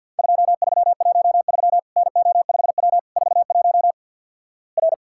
Morse code, heard as a keyed tone, about 52 wpm, 700 Hz. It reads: 2313 NO5W 41 R
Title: Worked All Europe DX Contest - QTC Exchange - 50wpm